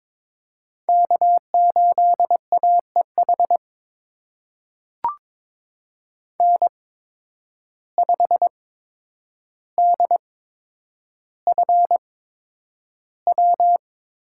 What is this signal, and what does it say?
Morse code, heard as a keyed tone, about 22 wpm, spaced as for 9 wpm, 700 Hz. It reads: K8AEH N5DFW
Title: USA Call Signs - 5 Characters - 8x Character Spacing - 22wpm